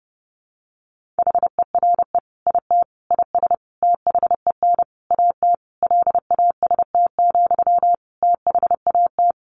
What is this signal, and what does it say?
Morse code, heard as a keyed tone, about 30 wpm, 700 Hz. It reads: HERE IT IS THEN AT LAST, THAT
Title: Sentences from Top 1500 Words 30wpm